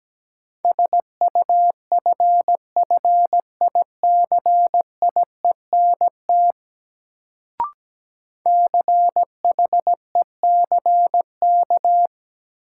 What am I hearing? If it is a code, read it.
Morse code, heard as a keyed tone, about 17 wpm, 700 Hz. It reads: SUFFICIENT CHECK
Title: Sentences from Top 1400 Words - New Word Drill Alt - 17wpm